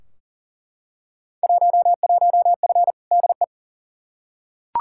Morse code, heard as a keyed tone, about 38 wpm, 700 Hz. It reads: 11F DE
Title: Field Day Exchanges Encore - 40wpm